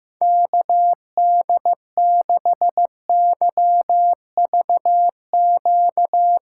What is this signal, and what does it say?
Morse code, heard as a keyed tone, about 15 wpm, 700 Hz. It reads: KD6YVQ
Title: USA Call Signs - 6 Characters - 8x Character Spacing - 15wpm